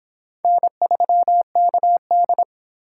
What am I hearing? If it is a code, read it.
Morse code, heard as a keyed tone, about 26 wpm, 700 Hz. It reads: N3KD